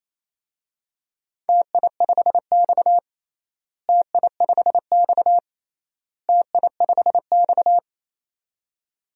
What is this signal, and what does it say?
Morse code, heard as a keyed tone, about 28 wpm, 700 Hz. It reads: TI5X TI5X TI5X
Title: DXCC-100 Call Signs - 4 Characters - Repeated 3x - 28wpm